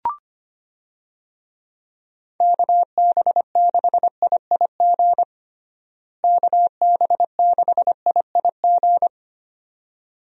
Morse code, heard as a keyed tone, about 25 wpm, 700 Hz. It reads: KB6IIG KB6IIG
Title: USA Call Signs - 6 Characters - Repeated 2x - 25wpm